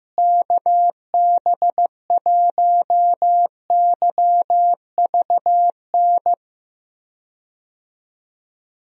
Morse code, heard as a keyed tone, about 16 wpm, 700 Hz. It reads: KB1YVN
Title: USA Call Signs - 6 Characters - Repeated 2x - 15wpm